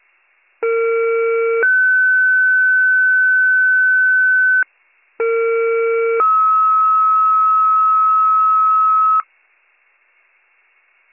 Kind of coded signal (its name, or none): two-tone page